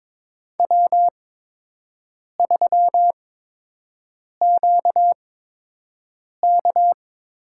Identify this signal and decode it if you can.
Morse code, heard as a keyed tone, about 22 wpm, spaced as for 6 wpm, 700 Hz. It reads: W3QK
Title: USA Call Signs - 8x Character Spacing - 22wpm